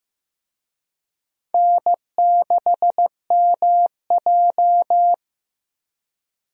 Morse code, heard as a keyed tone, about 15 wpm, 700 Hz. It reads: N6MJ
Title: USA Call Signs - 4 Characters - Repeated 3x - 15wpm